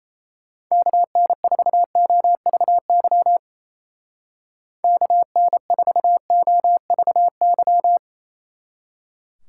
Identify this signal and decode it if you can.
Morse code, heard as a keyed tone, about 34 wpm, 700 Hz. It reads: KN4OVY KN4OVY
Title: Speed Racing - USA 2x3 Call Signs 28wpm